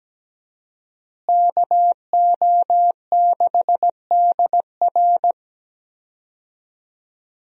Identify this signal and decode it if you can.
Morse code, heard as a keyed tone, about 17 wpm, 700 Hz. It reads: KO6DR
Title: USA Call Signs - 5 Characters - Repeated 2x - 17wpm